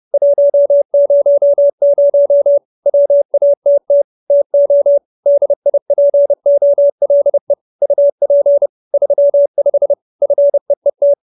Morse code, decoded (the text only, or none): 100 WATT TO DIPOLE UP 35 FEET